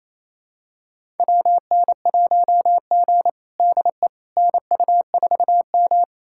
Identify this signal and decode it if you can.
Morse code, heard as a keyed tone, about 28 wpm, 700 Hz. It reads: WN1G DE NU4M
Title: USA Callsign DE Callsign - 4 Characters - 28wpm